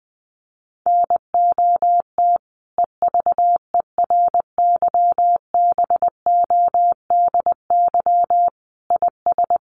Morse code, heard as a keyed tone, about 19 wpm, 700 Hz. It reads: NOT EVERYBODY IS
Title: Sentences from Top 1400 Words 20wpm